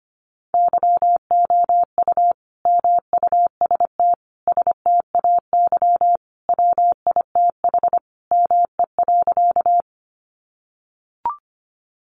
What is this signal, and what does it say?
Morse code, heard as a keyed tone, about 25 wpm, 700 Hz. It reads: YOU MUST STAY WITH ME.
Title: Sentences from Top 500 Words 25wpm